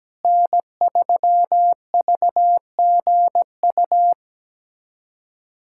Morse code, heard as a keyed tone, about 17 wpm, 700 Hz. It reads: N3VGU
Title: USA Callsign DE Callsign - 5 Characters - 17wpm